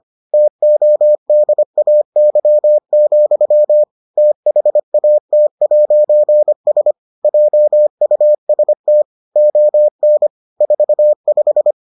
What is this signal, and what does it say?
Morse code, heard as a keyed tone, about 25 wpm, 600 Hz. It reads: TODAY, THAT'S JUST ON 45